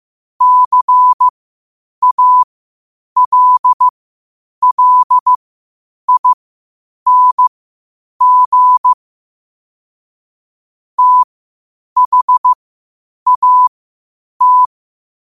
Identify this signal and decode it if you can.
Morse code, heard as a keyed tone, about 15 wpm, spaced as for 8 wpm, 1000 Hz. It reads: CALLING THAT
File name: qotdfiles / 8wpm